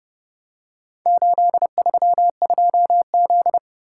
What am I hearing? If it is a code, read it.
Morse code, heard as a keyed tone, about 31 wpm, 700 Hz. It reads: 832Z